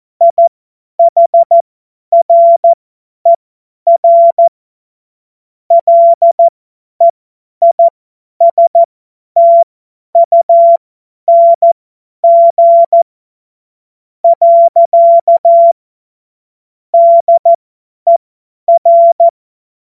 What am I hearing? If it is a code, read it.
Morse code, heard as a keyed tone, about 14 wpm, spaced as for 10 wpm, 680 Hz. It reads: IHRER LEISTUNG . DER
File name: Morsekurs / Witze / Witz 17. Januar 2023 50 F70